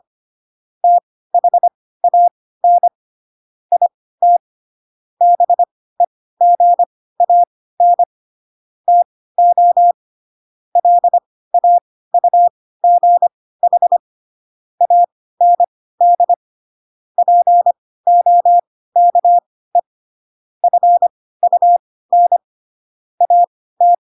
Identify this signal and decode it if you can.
Morse code, heard as a keyed tone, about 25 wpm, spaced as for 16 wpm, 700 Hz. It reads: THAN IT BEGAN TO LAUGH AND POKE FUN AT
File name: Pinocchio / Pinocchio ch3 10 25 ik1hge